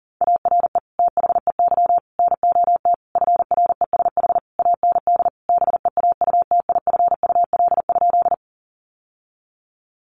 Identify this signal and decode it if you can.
Morse code, heard as a keyed tone, about 37 wpm, 700 Hz. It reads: ARE THEY NOT FRESH AND BEAUTIFUL?
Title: Sentences from Top 1200 Words 40wpm